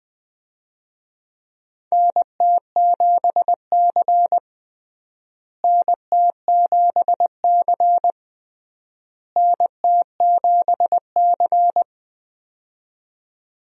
Morse code, heard as a keyed tone, about 20 wpm, 700 Hz. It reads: NT7C NT7C NT7C
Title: USA Call Signs - 4 Characters - Repeated 3x - 20wpm